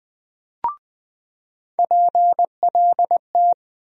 Morse code, heard as a keyed tone, about 20 wpm, 700 Hz. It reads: PLT
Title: USA Call Sign Postfixes - 3 Letters 20wpm